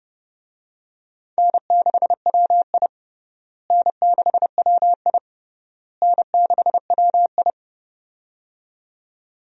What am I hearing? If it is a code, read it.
Morse code, heard as a keyed tone, about 29 wpm, 700 Hz. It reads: N6WI N6WI N6WI
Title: USA Call Signs - 4 Characters - Repeated 3x - 30wpm